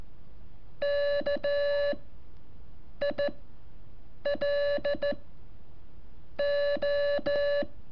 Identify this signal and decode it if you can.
Morse code, heard as a keyed tone, about 13 wpm, spaced as for 6 wpm, 580 Hz. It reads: KILO
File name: curso / ejercicio 17